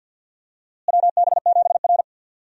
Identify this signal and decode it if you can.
Morse code, heard as a keyed tone, about 52 wpm, 700 Hz. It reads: WB7R
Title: USA Call Signs - 4 Characters - Repeated 2x - 50wpm